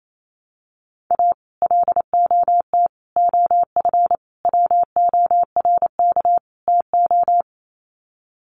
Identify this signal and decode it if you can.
Morse code, heard as a keyed tone, about 28 wpm, 700 Hz. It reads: A LOT OF WORK TO
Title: Sets of 5 Words Encore2 28wpm